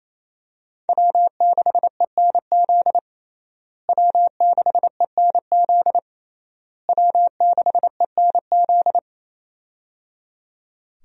Morse code, heard as a keyed tone, about 28 wpm, 700 Hz. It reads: W6ENZ W6ENZ W6ENZ
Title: USA Call Signs - 5 Characters - Repeated 3x - 28wpm